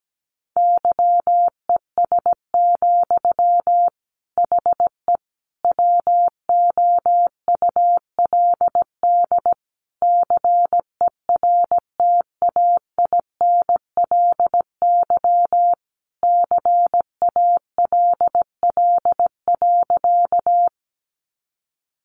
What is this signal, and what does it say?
Morse code, heard as a keyed tone, about 17 wpm, 700 Hz. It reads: YES, HE WOULD CERTAINLY CALL.
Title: Sentences from Top 900 Words 17wpm